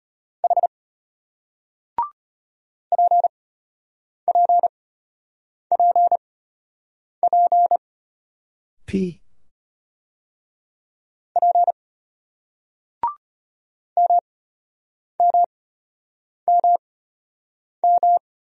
Morse code, heard as a keyed tone, about 35 wpm, spaced as for 8 wpm, 700 Hz. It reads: H PPPP P MMMM